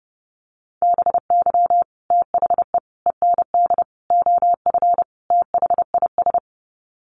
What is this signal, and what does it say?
Morse code, heard as a keyed tone, about 30 wpm, 700 Hz. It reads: BY THE END OF THIS
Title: Sets of 5 Words Encore 30wpm